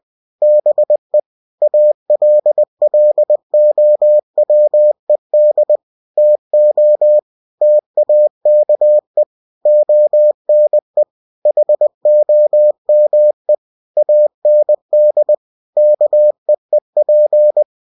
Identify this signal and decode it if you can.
Morse code, heard as a keyed tone, about 20 wpm, 600 Hz. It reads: BE ALLOWED TO TAKE ONE HOME AND KEEP
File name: cw-books / winnie-the-pooh / winnie-20wpm-page-0022